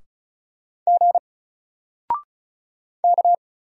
Morse code, heard as a keyed tone, about 36 wpm, 700 Hz. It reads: G K
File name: Single-Letter-Number-35wpm